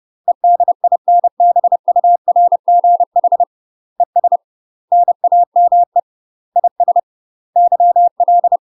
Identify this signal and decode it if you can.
Morse code, heard as a keyed tone, about 30 wpm, 700 Hz. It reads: EDINBURGH ES NAME IS YL
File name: UK MASTER QSOs TRAINING / 30WPM / UK MASTER QSOs 0025